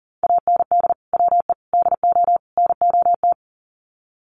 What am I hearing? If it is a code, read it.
Morse code, heard as a keyed tone, about 40 wpm, 700 Hz. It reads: AND WE DO NOT
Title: Sets of 4 Words 40wpm